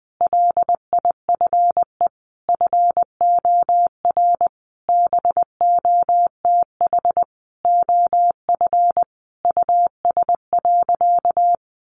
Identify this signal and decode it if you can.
Morse code, heard as a keyed tone, about 19 wpm, 700 Hz. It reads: LIFE FOR BOTH OF US.
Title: Sentences from Top 1100 Words 20wpm